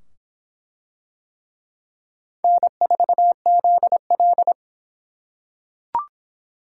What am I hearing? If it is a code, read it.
Morse code, heard as a keyed tone, about 25 wpm, 700 Hz. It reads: N4ZL